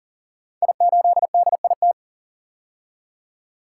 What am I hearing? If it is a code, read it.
Morse code, heard as a keyed tone, about 40 wpm, 700 Hz. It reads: I8DIT
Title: Code Groups - 5 Letter:Numbers - 40wpm